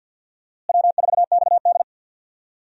Morse code, heard as a keyed tone, about 52 wpm, 700 Hz. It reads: W4XD
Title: USA Call Signs - 4 Characters - Repeated 3x - 50wpm